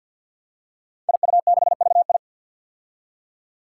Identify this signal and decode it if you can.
Morse code, heard as a keyed tone, about 46 wpm, 700 Hz. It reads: IU6VI
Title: Five Challenging Letter-Number Combinations 50wpm